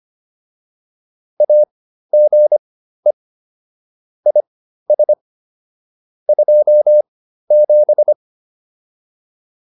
Morse code, heard as a keyed tone, about 25 wpm, spaced as for 13 wpm, 600 Hz. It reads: AGE IS 27
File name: QSO202 13